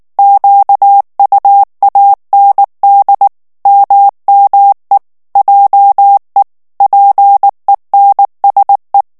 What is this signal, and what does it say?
Morse code, heard as a keyed tone, about 19 wpm, 800 Hz. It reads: QUAND MME JE PENSE
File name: morse mp3 / FRANCAIS TEXTE CLAIR 1 VITESSE 18 MOTS MINUTES